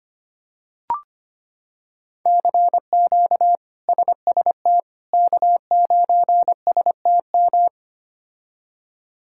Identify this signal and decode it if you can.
Morse code, heard as a keyed tone, about 25 wpm, 700 Hz. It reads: CQ SST K9STM
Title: K1USN SST QSO Practice - 25wpm